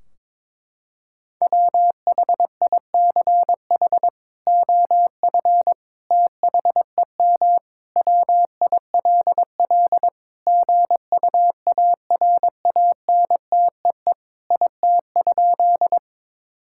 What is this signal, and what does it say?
Morse code, heard as a keyed tone, about 22 wpm, 700 Hz. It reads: WHICH OF THEM WILL GUARANTEE IT?
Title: Sentences from Top 1600 Words 22wpm